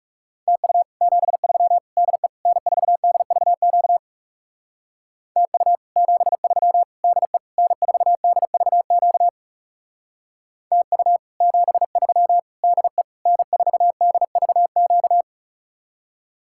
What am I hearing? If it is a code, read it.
Morse code, heard as a keyed tone, about 37 wpm, 700 Hz. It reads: TU 73 DE N4DVQ TU 73 DE N4DVQ TU 73 DE N4DVQ
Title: Speed-Racing Common QSO Phrases - 30wpm